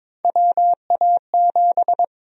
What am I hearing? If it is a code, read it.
Morse code, heard as a keyed tone, about 22 wpm, 700 Hz. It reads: WA7
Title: USA Call Sign Prefixes - 22wpm